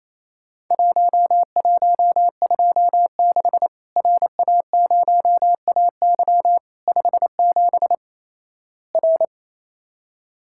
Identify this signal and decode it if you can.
Morse code, heard as a keyed tone, about 28 wpm, 700 Hz. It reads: 1126 RA0AY 57 R
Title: Worked All Europe DX Contest - QTC Exchange - 28wpm